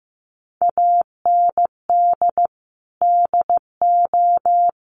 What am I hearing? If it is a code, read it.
Morse code, heard as a keyed tone, about 15 wpm, 700 Hz. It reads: AND DO
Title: Sentences from Top 600 Words 15wpm